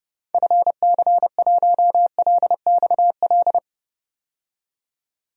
Morse code, heard as a keyed tone, about 30 wpm, 700 Hz. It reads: FC1LXL
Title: DXCC-100 Call Signs - 6 Characters - 30wpm